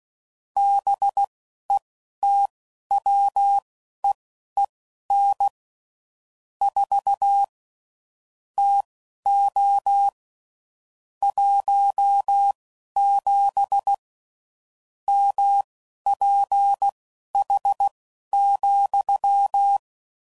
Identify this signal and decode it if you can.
Morse code, heard as a keyed tone, about 16 wpm, spaced as for 11 wpm, 800 Hz. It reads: BETWEEN 4 TO 17 MPH,
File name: CW Code Groups @10WPM / Weather Sets / Weather 4 21